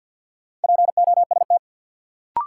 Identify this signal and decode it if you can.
Morse code, heard as a keyed tone, about 46 wpm, 700 Hz. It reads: POST
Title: Sentences from Top 1000 Words - New Word Drill - 50wpm